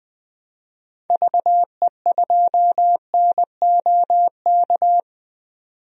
Morse code, heard as a keyed tone, about 20 wpm, 700 Hz. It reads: VE2NOK
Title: DXCC-100 Call Signs - 6 Characters - 20wpm